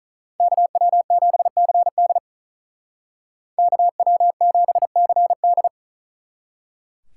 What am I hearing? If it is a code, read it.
Morse code, heard as a keyed tone, about 39 wpm, 700 Hz. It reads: KW7CD KW7CD
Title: Speed Racing - USA 2x2 Call Signs 35wpm